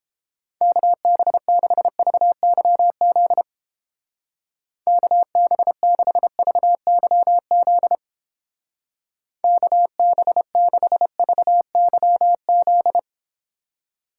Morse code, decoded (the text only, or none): KB6VYZ KB6VYZ KB6VYZ